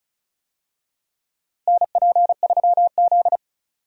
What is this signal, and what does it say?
Morse code, heard as a keyed tone, about 34 wpm, 700 Hz. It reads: NP3Z
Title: USA Call Signs - 4 Characters - Repeated 3x - 35wpm